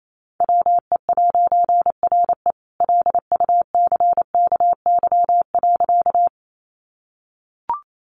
Morse code, heard as a keyed tone, about 28 wpm, 700 Hz. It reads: WE'RE LUCKY.
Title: A Sound Of Thunder 28wpm